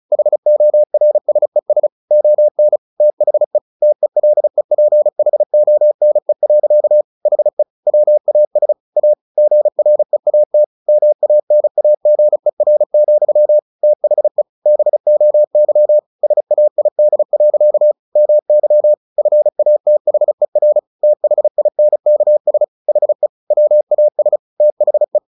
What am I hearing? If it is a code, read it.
Morse code, heard as a keyed tone, about 34 wpm, 600 Hz. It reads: HORSES ON THE TELEPHONE. HE WAS A GREAT MANAGER, THE BOY SAID. MY FATHER THINKS HE WAS THE
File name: the old man and the sea / the old man and the sea 1 35wpm